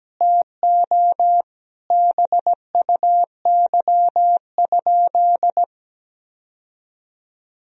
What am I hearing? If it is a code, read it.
Morse code, heard as a keyed tone, about 17 wpm, 700 Hz. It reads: TO BUY?
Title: Sentences from Top 1000 Words - Review Part 4 - 17wpm